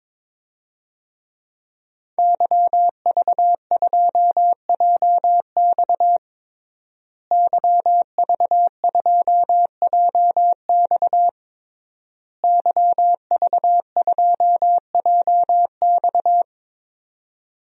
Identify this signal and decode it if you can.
Morse code, heard as a keyed tone, about 22 wpm, 700 Hz. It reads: YV2JX YV2JX YV2JX
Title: DXCC-100 Call Signs - 5 Characters - Repeated 3x - 22wpm